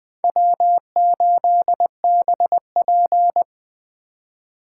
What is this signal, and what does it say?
Morse code, heard as a keyed tone, about 20 wpm, 700 Hz. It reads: W8BP
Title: Speed Racing - USA 1x2 Call Signs 17wpm